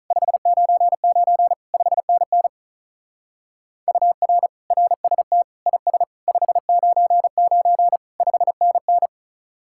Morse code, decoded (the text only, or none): 599 5NN UR RST IS 599 5NN